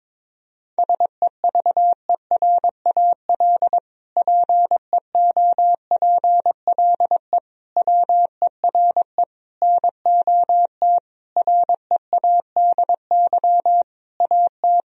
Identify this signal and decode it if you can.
Morse code, heard as a keyed tone, about 22 wpm, 700 Hz. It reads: SEVERAL PEOPLE WERE NOT READY AT
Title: Sentences from Top 1000 Words - Review Part 2 - 22wpm